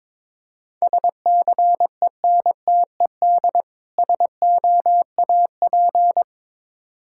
Morse code, heard as a keyed tone, about 22 wpm, 700 Hz. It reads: SCENTED SOAP
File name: Difficult Words with Hints - 22wpm